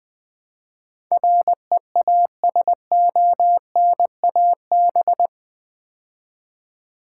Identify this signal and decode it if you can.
Morse code, heard as a keyed tone, about 20 wpm, 700 Hz. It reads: REASONAB
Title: Build A Word - 4 Letters At A Time - 10-Letter Words - 20wpm